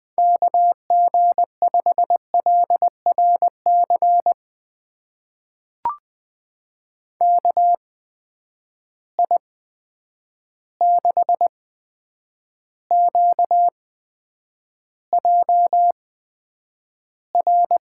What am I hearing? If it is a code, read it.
Morse code, heard as a keyed tone, about 20 wpm, spaced as for 9 wpm, 700 Hz. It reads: KG5LRC KI6QJR
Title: USA Call Signs - 6 Characters - 8x Character Spacing - 20wpm